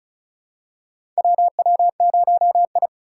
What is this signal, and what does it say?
Morse code, heard as a keyed tone, about 36 wpm, 690 Hz. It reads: WW0I
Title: USA Call Signs - 4 Characters - 3x Character Spacing - 35wpm